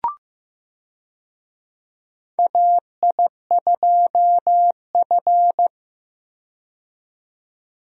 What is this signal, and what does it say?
Morse code, heard as a keyed tone, about 15 wpm, 700 Hz. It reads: AI2F
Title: USA Call Signs - 4 Characters - 15wpm